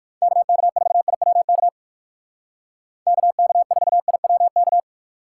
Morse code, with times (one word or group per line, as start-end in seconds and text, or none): 0.2-1.7 KK4IWK
3.1-4.8 KK4IWK